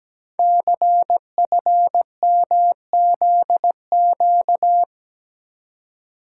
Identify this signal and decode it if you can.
Morse code, heard as a keyed tone, about 17 wpm, 700 Hz. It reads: CFMZQ
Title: Code Groups - 5 Letters - 17wpm